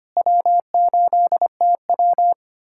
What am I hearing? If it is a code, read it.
Morse code, heard as a keyed tone, about 25 wpm, 700 Hz. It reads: W8TW
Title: USA Callsign DE Callsign - 4 Characters - 25wpm